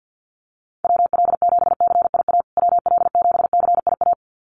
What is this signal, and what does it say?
Morse code, heard as a keyed tone, about 49 wpm, 700 Hz. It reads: WL7CIA WL7CIA
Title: USA Call Signs Repeated 2x 50wpm